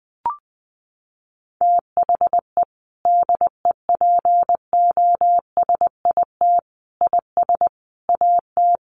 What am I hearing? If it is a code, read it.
Morse code, heard as a keyed tone, about 20 wpm, 700 Hz. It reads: THE DEPOSIT IS AT
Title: Sentences from Top 1300 Words 20wpm